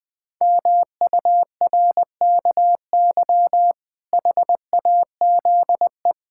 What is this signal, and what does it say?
Morse code, heard as a keyed tone, about 20 wpm, 700 Hz. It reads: MURKY HAZE
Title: Pangram Challenge v2 - 20wpm